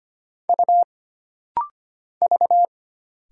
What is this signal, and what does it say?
Morse code, heard as a keyed tone, about 25 wpm, 700 Hz. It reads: U V
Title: Single Letters - Rapid-Fire 25wpm